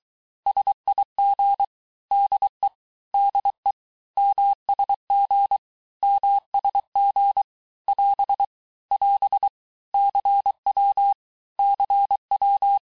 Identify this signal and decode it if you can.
Morse code, heard as a keyed tone, about 23 wpm, 800 Hz. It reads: SIG DE DE MSG MSG <AS> <AS> CW CW
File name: cours-du-lundi files / cours-1588-texte-abreviations-vitesse-17mpm-puis-22mpm